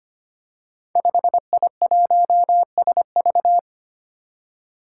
Morse code, heard as a keyed tone, about 25 wpm, 700 Hz. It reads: 5I1SV
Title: Five Challenging Letter-Number Combinations 25wpm